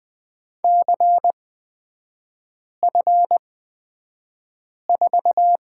Morse code, heard as a keyed tone, about 20 wpm, spaced as for 5 wpm, 700 Hz. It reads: CF4
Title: 224 Lesson 36e - BT - Three Characters 20wpm